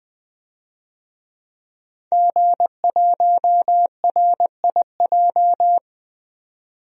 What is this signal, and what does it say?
Morse code, heard as a keyed tone, about 21 wpm, 700 Hz. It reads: G1RIJ
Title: DXCC-100 Call Signs - 5 Characters - 20wpm